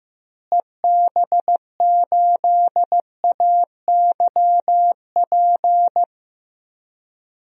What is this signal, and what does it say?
Morse code, heard as a keyed tone, about 15 wpm, 700 Hz. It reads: EB8AYP
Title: DXCC-100 Call Signs - 6 Characters - Repeated 3x - 15wpm